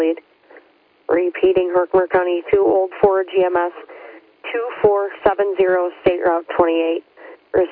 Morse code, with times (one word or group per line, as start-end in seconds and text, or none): none